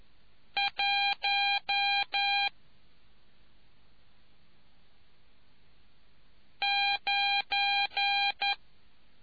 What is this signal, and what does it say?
Morse code, heard as a keyed tone, about 11 wpm, 2400 Hz. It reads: 1 9